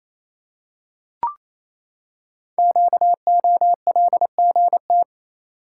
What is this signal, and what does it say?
Morse code, heard as a keyed tone, about 29 wpm, 700 Hz. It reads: QOLGT
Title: Code Groups - 5 Letters - 28wpm